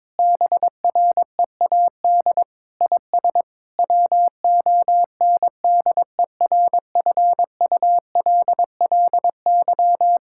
Morse code, heard as a keyed tone, about 22 wpm, 700 Hz. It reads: BREAD IS WONDERFULLY
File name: Difficult Sentences with Hints - 22wpm